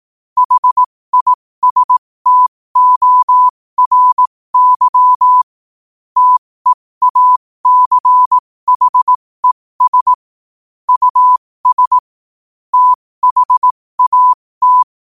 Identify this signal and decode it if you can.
Morse code, heard as a keyed tone, about 18 wpm, spaced as for 15 wpm, 1000 Hz. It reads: HISTORY TEACHES US THAT
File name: qotdfiles / 15wpm